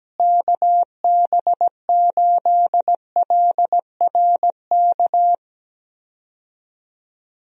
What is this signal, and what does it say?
Morse code, heard as a keyed tone, about 17 wpm, 700 Hz. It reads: KB8LRK
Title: USA Call Signs - 6 Characters - Repeated 3x - 17wpm